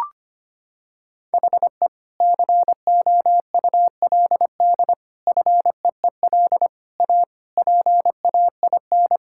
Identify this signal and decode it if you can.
Morse code, heard as a keyed tone, about 25 wpm, 700 Hz. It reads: HE COULD FEEL A PAIN
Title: Sentences from Top 1600 Words 25wpm